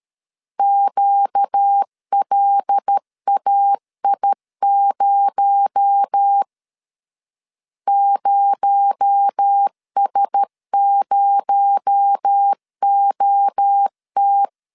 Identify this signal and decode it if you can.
Morse code, heard as a keyed tone, about 13 wpm, 800 Hz. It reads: QLAI0 0S0OT
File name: LEZIONI / 6 PKX0 / 6 PKX0 12wpm 2231